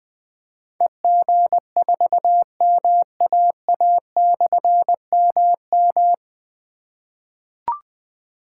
Morse code, heard as a keyed tone, about 20 wpm, 700 Hz. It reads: EG4MAA/MM
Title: Difficult Call Signs 20wpm